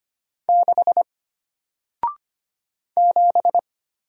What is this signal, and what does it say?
Morse code, heard as a keyed tone, about 25 wpm, 700 Hz. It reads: 6 7
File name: Single-Letter-Number-25wpm